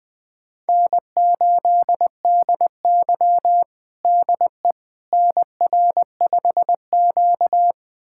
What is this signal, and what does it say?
Morse code, heard as a keyed tone, about 20 wpm, 700 Hz. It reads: N8DY DE NR5Q
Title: USA Callsign DE Callsign - 4 Characters - 20wpm